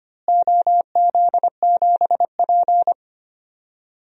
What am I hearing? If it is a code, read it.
Morse code, heard as a keyed tone, about 25 wpm, 700 Hz. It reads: OZ7P
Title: DXCC-100 Call Signs - 4 Characters - Repeated 2x - 25wpm